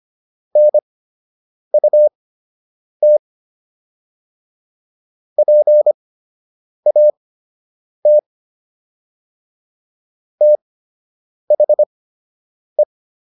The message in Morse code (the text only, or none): NUT PAT THE